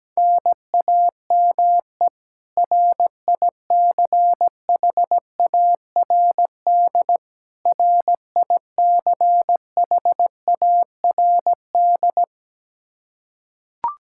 Morse code, heard as a keyed tone, about 17 wpm, 700 Hz. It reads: NAME RICHARD RICHARD
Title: Practice QSO - Exchange 1 - Beginning - 17wpm